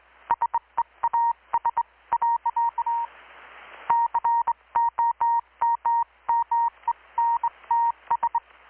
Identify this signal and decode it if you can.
Morse code, heard as a keyed tone, about 21 wpm, 930 Hz. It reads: SEAS. COMMENTS
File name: blog / swl / 6-abril-2019-12695-USB-KPS